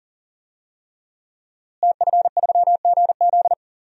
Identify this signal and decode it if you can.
Morse code, heard as a keyed tone, about 40 wpm, 700 Hz. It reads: TF3GZ
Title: DXCC-100 Call Signs - 5 Characters - Repeated 3x - 40wpm